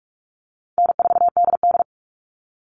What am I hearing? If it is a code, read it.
Morse code, heard as a keyed tone, about 44 wpm, 700 Hz. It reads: N4DD
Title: All Cut Numbers - 4-Digit 45wpm